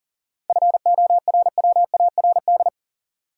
Morse code, heard as a keyed tone, about 40 wpm, 700 Hz. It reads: FORWARD
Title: Sentences from Top 1300 Words - New Word Drill Alt - 40wpm